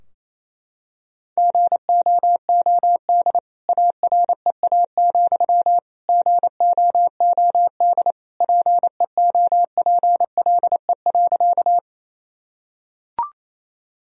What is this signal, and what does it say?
Morse code, heard as a keyed tone, about 28 wpm, 700 Hz. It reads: GOOD AREA, GOOD PEOPLE.
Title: Sentences from Top 1000 Words - Review Part 2 - 28wpm